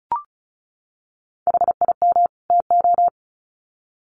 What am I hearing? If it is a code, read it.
Morse code, heard as a keyed tone, about 34 wpm, 700 Hz. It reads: HIM TO
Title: Sets of 2 Words 35wpm